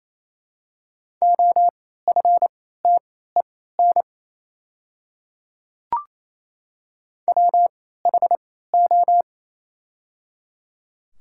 Morse code, heard as a keyed tone, about 27 wpm, spaced as for 16 wpm, 700 Hz. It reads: OFTEN WHO
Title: 3x Character Spacing - Top 500 Words - 28wpm